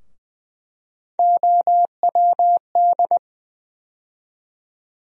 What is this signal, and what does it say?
Morse code, heard as a keyed tone, about 20 wpm, 700 Hz. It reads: OWD
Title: USA Call Sign Postfixes - 3 Letters 20wpm